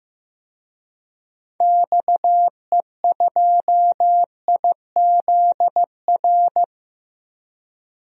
Morse code, heard as a keyed tone, about 16 wpm, 700 Hz. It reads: XE2IZR
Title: DXCC-100 Call Signs - 6 Characters - Repeated 2x - 15wpm